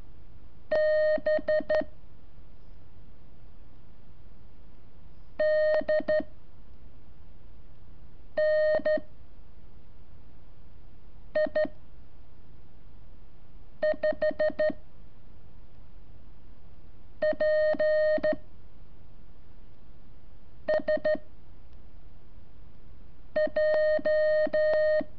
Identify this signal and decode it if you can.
Morse code, heard as a keyed tone, about 13 wpm, 620 Hz. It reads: B D N I 5 P S J
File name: curso / ejercicio 25